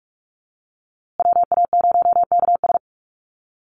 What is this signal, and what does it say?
Morse code, heard as a keyed tone, about 44 wpm, 690 Hz. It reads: WA0KS
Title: USA Call Signs Repeated 3x 45wpm